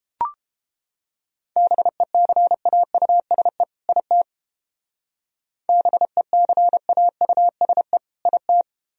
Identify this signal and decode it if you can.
Morse code, heard as a keyed tone, about 29 wpm, 700 Hz. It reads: BECAUSE IT BECAUSE IT